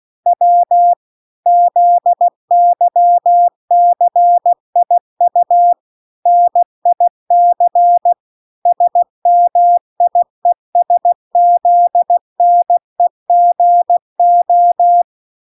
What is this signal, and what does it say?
Morse code, heard as a keyed tone, about 16 wpm, 700 Hz. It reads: W ZYCIU NIC SMIESZNEGO
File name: 16wpm